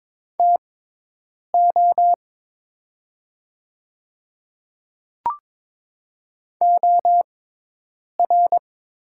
Morse code, heard as a keyed tone, about 23 wpm, spaced as for 8 wpm, 700 Hz. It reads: TO OR
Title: 6x Character Spacing - Top 200 Words - 22wpm